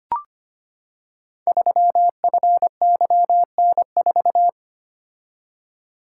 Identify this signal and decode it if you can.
Morse code, heard as a keyed tone, about 25 wpm, 700 Hz. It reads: 3FYN4
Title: Code Groups - 5 Letter:Numbers - 25wpm